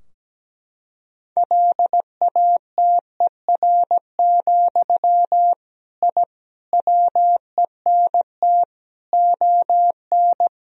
Morse code, heard as a keyed tone, about 17 wpm, 700 Hz. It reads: LATER, I WENT ON
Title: Sentences from Top 1000 Words - Review Part 4 - 17wpm